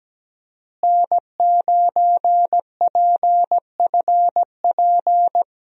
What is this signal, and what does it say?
Morse code, heard as a keyed tone, about 17 wpm, 700 Hz. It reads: N9PFP
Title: USA Call Signs - 5 Characters - Repeated 2x - 17wpm